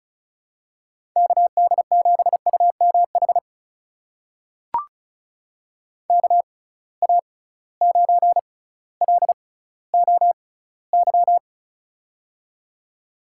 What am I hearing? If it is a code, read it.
Morse code, heard as a keyed tone, about 34 wpm, spaced as for 18 wpm, 700 Hz. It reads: KD7UMH KA9LOY
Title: USA Call Signs - 6 Characters - 6x Character Spacing - 35wpm